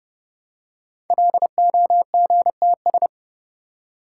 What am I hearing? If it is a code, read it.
Morse code, heard as a keyed tone, about 30 wpm, 700 Hz. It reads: LOGTS
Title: Code Groups - 5 Letter:Numbers - 30wpm